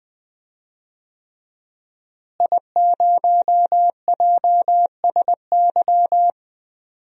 Morse code, heard as a keyed tone, about 20 wpm, 700 Hz. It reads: I0JSY
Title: DXCC-100 Call Signs - 5 Characters - Repeated 3x - 20wpm